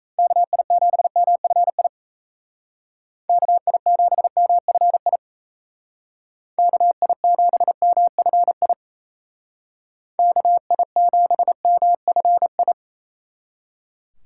Morse code, heard as a keyed tone, about 36 wpm, 700 Hz. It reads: KI7MFI KI7MFI KI7MFI KI7MFI